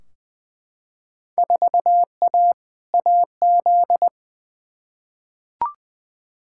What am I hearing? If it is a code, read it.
Morse code, heard as a keyed tone, about 20 wpm, 700 Hz. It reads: 4A AZ
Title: Field Day Exchanges Encore - 20wpm